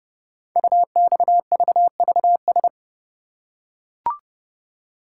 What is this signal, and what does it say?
Morse code, heard as a keyed tone, about 30 wpm, 700 Hz. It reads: UXVVS
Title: Code Groups - 5 Letters - 30wpm